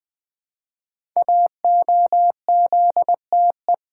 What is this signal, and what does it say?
Morse code, heard as a keyed tone, about 20 wpm, 700 Hz. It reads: AOZTE
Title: Code Groups - 5 Letters - 20wpm